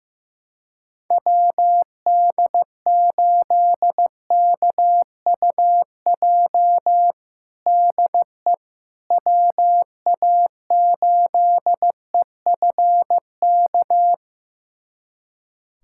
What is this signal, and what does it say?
Morse code, heard as a keyed tone, about 16 wpm, 700 Hz. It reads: WD8KUJ DE WA8EFK
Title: USA Callsign DE Callsign 15wpm